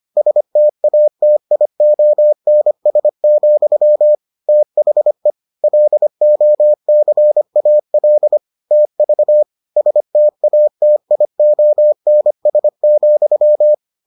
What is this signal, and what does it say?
Morse code, heard as a keyed tone, about 25 wpm, 600 Hz. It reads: STATIONS, THE LOCAL TV STATIONS,